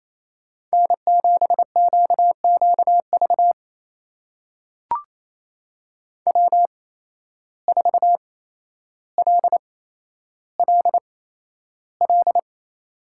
Morse code, heard as a keyed tone, about 28 wpm, spaced as for 12 wpm, 700 Hz. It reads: N7QQV W4LLL
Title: USA Call Signs - 5 Characters - 8x Character Spacing - 28wpm